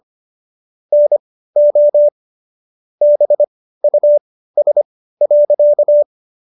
Morse code, heard as a keyed tone, about 25 wpm, spaced as for 15 wpm, 590 Hz. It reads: NO BUS.